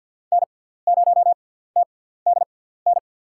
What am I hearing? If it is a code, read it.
Morse code, heard as a keyed tone, about 52 wpm, spaced as for 17 wpm, 700 Hz. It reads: N0TDN
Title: USA Call Signs - 6x Character Spacing - 50wpm